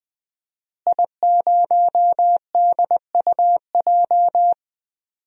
Morse code, heard as a keyed tone, about 20 wpm, 700 Hz. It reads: I0DUJ
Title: DXCC-100 Call Signs - 5 Characters - Repeated 2x - 20wpm